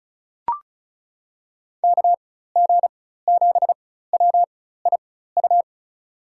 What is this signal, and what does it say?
Morse code, heard as a keyed tone, about 34 wpm, spaced as for 16 wpm, 700 Hz. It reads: KG7WIU
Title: USA Call Signs - 6 Characters - 4x Character Spacing - 35wpm